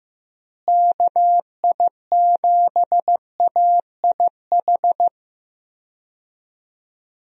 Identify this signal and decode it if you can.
Morse code, heard as a keyed tone, about 15 wpm, 700 Hz. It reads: KI7AIH
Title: USA Call Signs - 6 Characters - Repeated 3x - 15wpm